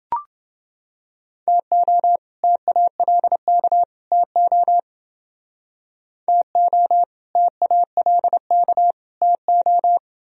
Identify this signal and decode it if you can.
Morse code, heard as a keyed tone, about 30 wpm, 700 Hz. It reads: TO TALK TO TO TALK TO